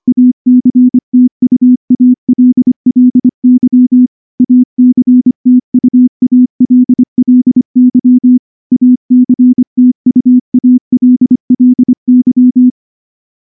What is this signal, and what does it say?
Morse code, heard as a keyed tone, about 24 wpm, 250 Hz. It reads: ACTUALLY ACTUALLY ACTUALLY
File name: morse-training-25WPM